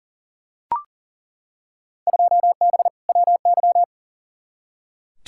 Morse code, heard as a keyed tone, about 40 wpm, 700 Hz. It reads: 2B WY
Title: Field Day Exchanges Encore - 40wpm